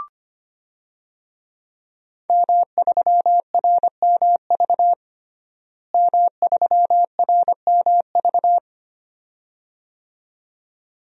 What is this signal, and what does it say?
Morse code, heard as a keyed tone, about 25 wpm, 700 Hz. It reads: M3RMV M3RMV
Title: DXCC-100 Call Signs - 5 Characters - Repeated 2x - 25wpm